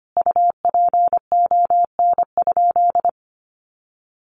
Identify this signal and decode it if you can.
Morse code, heard as a keyed tone, about 25 wpm, 700 Hz. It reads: UPON?
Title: Sentences from Top 1100 Words 25wpm